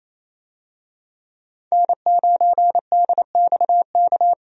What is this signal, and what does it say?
Morse code, heard as a keyed tone, about 28 wpm, 700 Hz. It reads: N9DXK
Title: USA Call Signs - 5 Characters - 28wpm